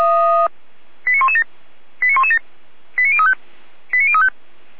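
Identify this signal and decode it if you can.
Five-tone selective call (ZVEI), heard as quick runs of five tones, tones of about 70 ms. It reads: 89117 89117 89035 89035
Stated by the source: Five-tone ZVEI selective call received on an SDR.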